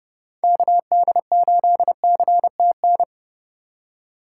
Morse code, heard as a keyed tone, about 30 wpm, 700 Hz. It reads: KD8CTN
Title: USA Call Signs - 3x Character Spacing - 30wpm